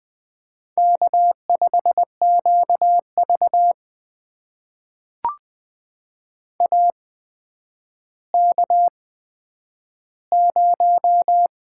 Morse code, heard as a keyed tone, about 20 wpm, spaced as for 10 wpm, 700 Hz. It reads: K5QV AK0
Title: USA Call Signs - 4 Characters - 8x Character Spacing - 20wpm